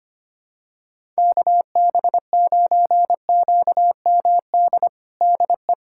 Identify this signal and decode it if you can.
Morse code, heard as a keyed tone, about 25 wpm, 700 Hz. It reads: KB9QMD DE
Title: USA Callsign DE Callsign 25wpm